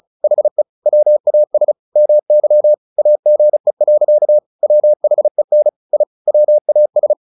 Morse code, heard as a keyed tone, about 34 wpm, 600 Hz. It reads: HE WAS MY AGE. WHEN I WAS
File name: the old man and the sea / the old man and the sea 1 35wpm